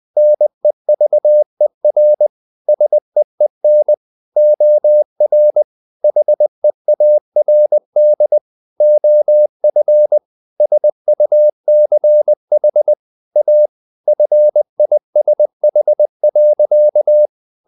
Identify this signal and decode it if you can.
Morse code, heard as a keyed tone, about 20 wpm, 600 Hz. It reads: NEVER SEEN OR HEARD OF SUCH A FISH.
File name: the old man and the sea part 2 20wpm